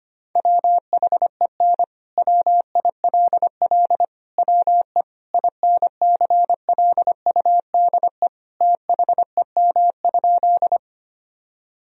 Morse code, heard as a keyed tone, about 25 wpm, 700 Hz. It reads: WHEN WILL WE INCLUDE THEM?
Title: Sentences from Top 1000 Words - Review Part 3 - 25wpm